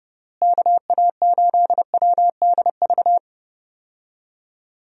Morse code, heard as a keyed tone, about 32 wpm, 700 Hz. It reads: KA8WDV
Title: USA Call Signs - 6 Characters - Repeated 3x - 30wpm